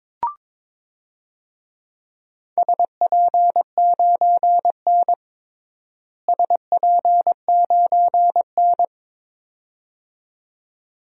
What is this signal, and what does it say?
Morse code, heard as a keyed tone, about 22 wpm, 700 Hz. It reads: SP9N SP9N
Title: DXCC-100 Call Signs - 4 Characters - Repeated 2x - 22wpm